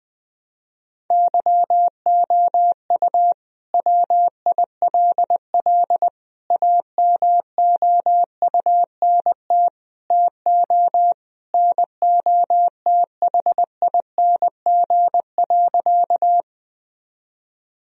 Morse code, heard as a keyed tone, about 20 wpm, 700 Hz. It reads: YOU WILL AMOUNT TO NOTHING.
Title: Sentences from Top 1000 Words - Review Part 4 - 20wpm